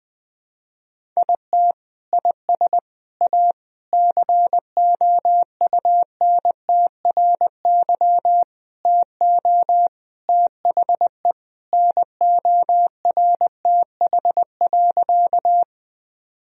Morse code, heard as a keyed tone, about 20 wpm, 700 Hz. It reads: IT IS A COUNTRY TO THE NORTH.
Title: Sentences from Top 1000 Words - Review Part 1 - 20wpm